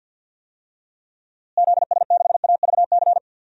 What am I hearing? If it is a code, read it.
Morse code, heard as a keyed tone, about 46 wpm, 700 Hz. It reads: ZS6AVC
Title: DXCC-100 Call Signs - 6 Characters - Repeated 3x - 50wpm